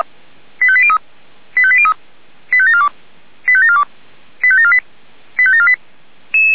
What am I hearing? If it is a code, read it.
Five-tone selective call (ZVEI), heard as quick runs of five tones, tones of about 70 ms. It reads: 85803 85803 85742 85742 85748 85748